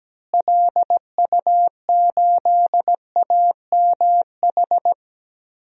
Morse code, heard as a keyed tone, about 17 wpm, 700 Hz. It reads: LU8AMH
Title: DXCC-100 Call Signs - 6 Characters - Repeated 2x - 17wpm